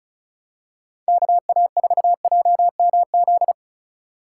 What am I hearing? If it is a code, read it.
Morse code, heard as a keyed tone, about 36 wpm, 700 Hz. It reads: KA4JMZ
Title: USA Call Signs - 6 Characters - 2x Character Spacing - 35wpm